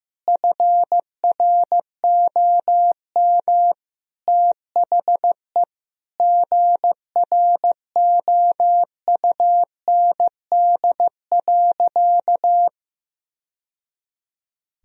Morse code, heard as a keyed tone, about 15 wpm, 700 Hz. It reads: FROM THE GROUND.
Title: Sentences from Top 1000 Words - Review Part 2 - 15wpm